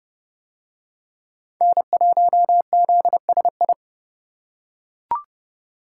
Morse code, heard as a keyed tone, about 28 wpm, 700 Hz. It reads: N1ZSI